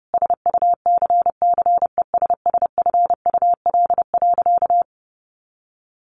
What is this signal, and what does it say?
Morse code, heard as a keyed tone, about 30 wpm, 700 Hz. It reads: SUCCESSFUL.
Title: Sentences from Top 1100 Words 30wpm